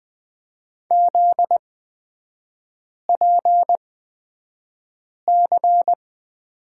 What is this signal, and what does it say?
Morse code, heard as a keyed tone, about 20 wpm, spaced as for 5 wpm, 700 Hz. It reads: ZPC
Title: 243 Lesson 39e - Letter z - Three Characters 20wpm